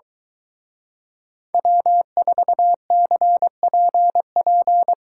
Morse code, heard as a keyed tone, about 23 wpm, 700 Hz. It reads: W4CPP